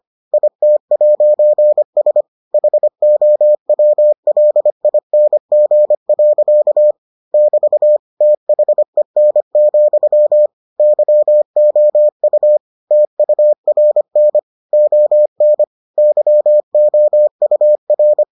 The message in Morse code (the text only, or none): IT'S HOWLING. <BT> THEN, YOU TURN ON YOUR